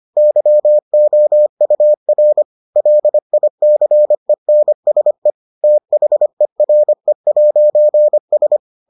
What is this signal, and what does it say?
Morse code, heard as a keyed tone, about 25 wpm, 600 Hz. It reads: YOUR LICENSE THERE'S